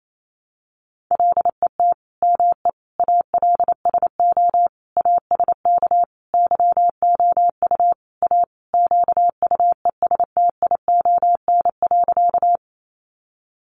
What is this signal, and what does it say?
Morse code, heard as a keyed tone, about 28 wpm, 700 Hz. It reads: LET ME ALSO ASK YOU A QUESTION.
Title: Sentences from Top 400 Words 28wpm